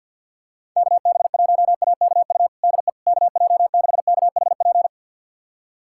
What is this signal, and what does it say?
Morse code, heard as a keyed tone, about 51 wpm, 700 Hz. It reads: KB1AKU DE KJ6KHP
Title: USA Callsign DE Callsign - 6 Characters - 50wpm